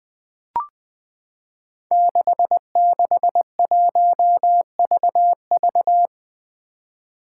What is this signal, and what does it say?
Morse code, heard as a keyed tone, about 20 wpm, 700 Hz. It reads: 661VV
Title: Five Challenging Letter-Number Combinations 20wpm